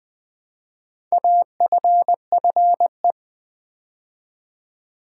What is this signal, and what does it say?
Morse code, heard as a keyed tone, about 20 wpm, 700 Hz. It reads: AFFE
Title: Build A Word - 2 Letters At A Time - 6-Letter Words - 20wpm